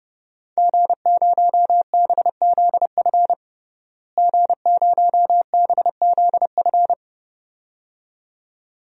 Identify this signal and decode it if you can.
Morse code, heard as a keyed tone, about 30 wpm, 700 Hz. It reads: G0BZF G0BZF
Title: DXCC-100 Call Signs - 5 Characters - Repeated 2x - 30wpm